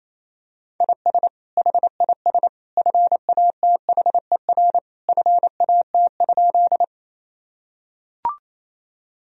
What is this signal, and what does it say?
Morse code, heard as a keyed tone, about 28 wpm, 700 Hz. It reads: IS HIS FATHER FAT?
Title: Sentences from Top 1000 Words - Review Part 4 - 28wpm